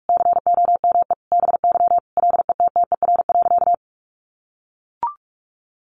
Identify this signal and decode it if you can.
Morse code, heard as a keyed tone, about 44 wpm, 700 Hz. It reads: COME BY LETTER.
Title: Sentences from Top 1400 Words 45wpm